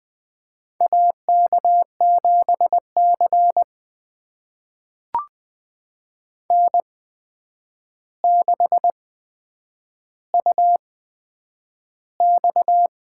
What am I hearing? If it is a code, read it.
Morse code, heard as a keyed tone, about 20 wpm, spaced as for 9 wpm, 700 Hz. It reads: AK7C N6UX
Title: USA Call Signs - 4 Characters - 8x Character Spacing - 20wpm